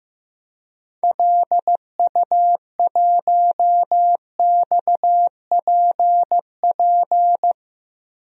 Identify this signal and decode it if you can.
Morse code, heard as a keyed tone, about 16 wpm, 700 Hz. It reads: LU1XPP
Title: DXCC-100 Call Signs - 6 Characters - 15wpm